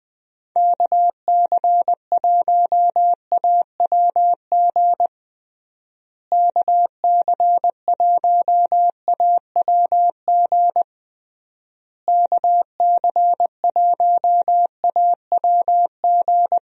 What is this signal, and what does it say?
Morse code, heard as a keyed tone, about 20 wpm, 700 Hz. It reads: KC1AWG KC1AWG KC1AWG
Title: USA Call Signs - 6 Characters - Repeated 3x - 20wpm